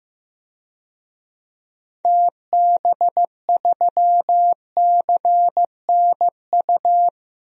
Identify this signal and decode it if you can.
Morse code, heard as a keyed tone, about 15 wpm, 700 Hz. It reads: TB3CNU
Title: DXCC-100 Call Signs - 6 Characters - 15wpm